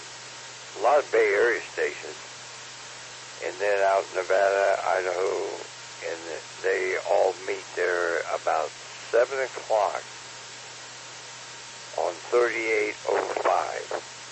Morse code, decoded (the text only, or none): none